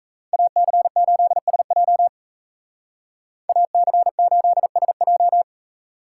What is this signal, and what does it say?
Morse code, heard as a keyed tone, about 33 wpm, 700 Hz. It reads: AC8SJ AC8SJ
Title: Speed Racing - USA 2x2 Call Signs 28wpm